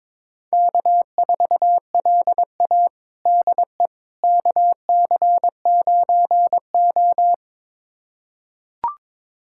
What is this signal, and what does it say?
Morse code, heard as a keyed tone, about 22 wpm, 700 Hz. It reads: K4LA DE KC9O
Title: USA Callsign DE Callsign - 4 Characters - 22wpm